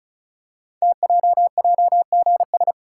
Morse code, heard as a keyed tone, about 36 wpm, 700 Hz. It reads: TJJGS
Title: Code Groups - 5 Letters - 35wpm